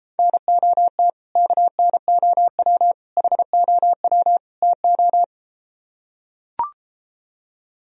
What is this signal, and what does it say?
Morse code, heard as a keyed tone, about 33 wpm, 700 Hz. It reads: NOT KNOW HOW TO